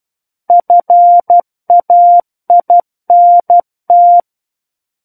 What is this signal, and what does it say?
Morse code, heard as a keyed tone, about 12 wpm, 700 Hz. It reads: FAINT
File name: Book One / 03-Chapter 3 - 12 WPM